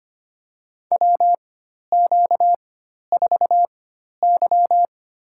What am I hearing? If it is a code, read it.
Morse code, heard as a keyed tone, about 25 wpm, spaced as for 12 wpm, 690 Hz. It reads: WQ4Y
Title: USA Call Signs - 4 Characters - 4x Character Spacing - 25wpm